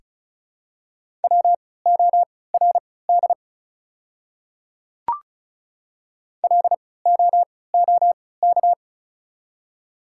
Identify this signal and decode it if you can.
Morse code, heard as a keyed tone, about 36 wpm, spaced as for 20 wpm, 700 Hz. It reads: WORD LOOK
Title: 3x Character Spacing - Top 100 Words - 35wpm